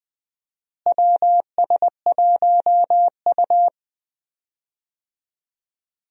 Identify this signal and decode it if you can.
Morse code, heard as a keyed tone, about 21 wpm, 700 Hz. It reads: WS1U
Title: USA Call Signs - 4 Characters - Repeated 2x - 20wpm